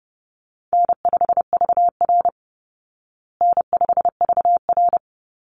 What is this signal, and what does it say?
Morse code, heard as a keyed tone, about 30 wpm, 700 Hz. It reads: N5VR N5VR
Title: Easy Call Signs Repeated 2x - 30wpm